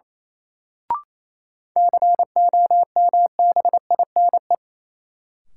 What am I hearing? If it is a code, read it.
Morse code, heard as a keyed tone, about 28 wpm, 700 Hz. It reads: COMBINE
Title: Sentences from Top 1600 Words - New Word Drill - 28wpm